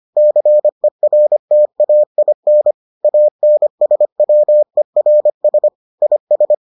Morse code, heard as a keyed tone, about 25 wpm, 600 Hz. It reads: CERTAIN ANSWERS IS